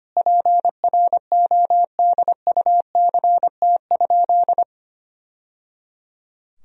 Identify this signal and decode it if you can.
Morse code, heard as a keyed tone, about 25 wpm, 700 Hz. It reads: PRODUCT?
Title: Sentences from Top 1000 Words - Review Part 2 - 25wpm